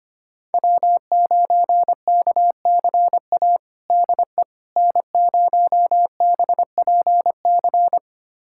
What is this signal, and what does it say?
Morse code, heard as a keyed tone, about 25 wpm, 700 Hz. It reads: W9KCA DE N0BPC
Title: USA Callsign DE Callsign - 5 Characters - 25wpm